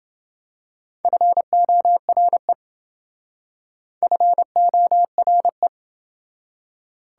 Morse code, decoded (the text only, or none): FORE FORE